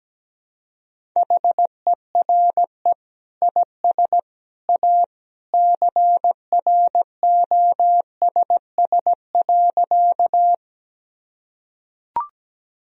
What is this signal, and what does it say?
Morse code, heard as a keyed tone, about 17 wpm, 700 Hz. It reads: HERE IS A CROSS.
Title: Sentences from Top 1000 Words - Review Part 1 - 17wpm